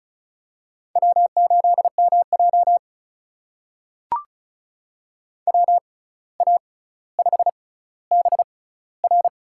Morse code, heard as a keyed tone, about 36 wpm, spaced as for 17 wpm, 700 Hz. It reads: W8MJ WA5BR
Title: USA Call Signs - 6x Character Spacing - 35wpm